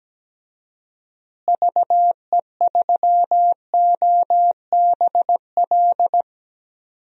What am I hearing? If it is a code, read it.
Morse code, heard as a keyed tone, about 17 wpm, 700 Hz. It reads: VE3OBL
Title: DXCC-100 Call Signs - 6 Characters - Repeated 2x - 17wpm